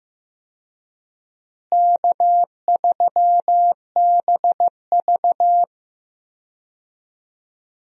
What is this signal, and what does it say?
Morse code, heard as a keyed tone, about 15 wpm, 700 Hz. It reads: K3BV
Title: USA Call Signs - 4 Characters - Repeated 3x - 15wpm